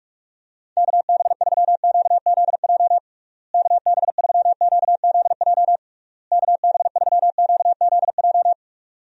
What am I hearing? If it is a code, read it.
Morse code, heard as a keyed tone, about 41 wpm, 700 Hz. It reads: KB3QZJ KB3QZJ KB3QZJ
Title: USA Call Signs - 6 Characters - Repeated 3x - 45wpm